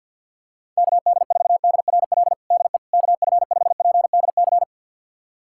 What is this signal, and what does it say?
Morse code, heard as a keyed tone, about 46 wpm, 700 Hz. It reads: KD4DRL DE KF5PDC
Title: USA Callsign DE Callsign - 6 Characters - 50wpm